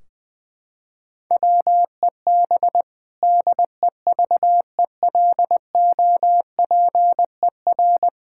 Morse code, heard as a keyed tone, about 21 wpm, 700 Hz. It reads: WEB DEVELOPER
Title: K1USN SST QSO Practice 20wpm